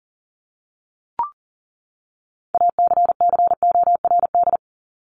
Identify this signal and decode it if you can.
Morse code, heard as a keyed tone, about 40 wpm, 700 Hz. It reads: ACCORD
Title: ING Suffix - Top 100 Words with Base Words 40wpm